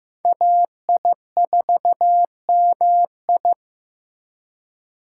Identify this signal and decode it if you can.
Morse code, heard as a keyed tone, about 15 wpm, 700 Hz. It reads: AI4MI
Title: K1USN SST QSO Practice - 15wpm